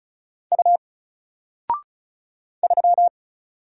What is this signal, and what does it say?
Morse code, heard as a keyed tone, about 36 wpm, 700 Hz. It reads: U 3
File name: Single-Letter-Number-35wpm